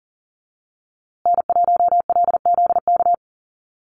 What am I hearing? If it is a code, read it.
Morse code, heard as a keyed tone, about 40 wpm, 700 Hz. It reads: N1LZK
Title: USA Call Signs Repeated 2x 40wpm